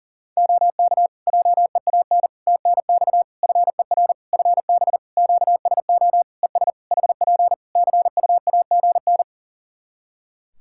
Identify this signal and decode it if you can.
Morse code, heard as a keyed tone, about 40 wpm, 700 Hz. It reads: OK JEAN TNX FER FB QSO ES HP CUAGN
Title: Practice QSO - Exchange 4 - Ending - 40wpm